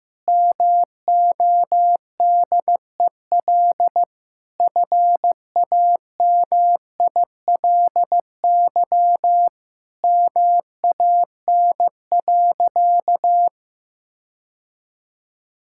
Morse code, heard as a keyed tone, about 15 wpm, 700 Hz. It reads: MODEL FAMILY MAN.
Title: Sentences from Top 1000 Words - Review Part 5 - 15wpm